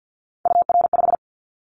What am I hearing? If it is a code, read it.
Morse code, heard as a keyed tone, about 50 wpm, 700 Hz. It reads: UR5
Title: Callsign Prefixes for Top 25 DXCC - 50wpm